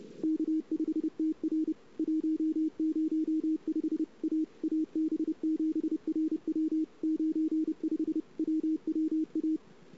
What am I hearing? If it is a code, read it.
Morse code, heard as a keyed tone, about 30 wpm, 320 Hz. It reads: K5TR 105 A AB7RW 95 WWA
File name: CW / 2008-ss-cw-k5tr-op-n5rz -23:00